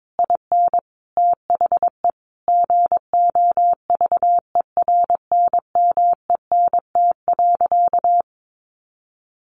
Morse code, heard as a keyed tone, about 22 wpm, 700 Hz. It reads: IN THE GOVERNMENT.
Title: Sentences from Top 1300 Words 22wpm